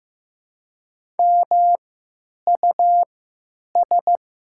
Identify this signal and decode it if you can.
Morse code, heard as a keyed tone, about 15 wpm, spaced as for 9 wpm, 700 Hz. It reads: MUS
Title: 3x Character Spacing - Top 200 Words - 15wpm